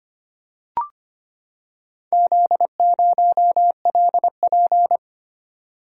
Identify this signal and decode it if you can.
Morse code, heard as a keyed tone, about 25 wpm, 700 Hz. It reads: Z0LP
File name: practice / 25wpm / Four Letter-Number Combinations - 25wpm